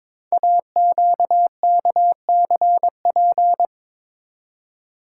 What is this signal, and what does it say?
Morse code, heard as a keyed tone, about 22 wpm, 700 Hz. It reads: AQKCP
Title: Code Groups - 5 Letters - 22wpm